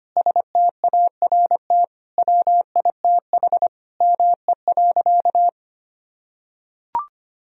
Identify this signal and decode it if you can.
Morse code, heard as a keyed tone, about 25 wpm, 700 Hz. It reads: START WITH ME.
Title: Sentences from Top 1000 Words - Review Part 1 - 25wpm